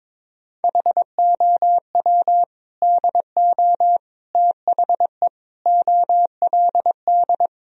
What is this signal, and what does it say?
Morse code, heard as a keyed tone, about 22 wpm, 700 Hz. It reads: HOW DO THE OLD
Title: Sentences from Top 1000 Words - Review Part 1 - 22wpm